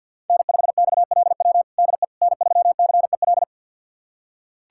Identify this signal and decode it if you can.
Morse code, heard as a keyed tone, about 47 wpm, 700 Hz. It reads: N5XLW DE N3CEL
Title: USA Callsign DE Callsign 50wpm